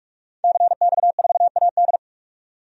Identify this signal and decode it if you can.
Morse code, heard as a keyed tone, about 46 wpm, 700 Hz. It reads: CX4AD
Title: DXCC-100 Call Signs - 5 Characters - 45wpm